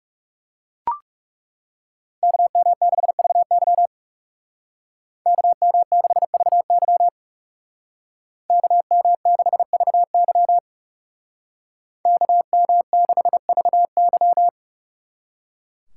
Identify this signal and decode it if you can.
Morse code, heard as a keyed tone, about 37 wpm, 700 Hz. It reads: KM6VY KM6VY KM6VY KM6VY
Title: Speed Racing - USA Call Signs 30wpm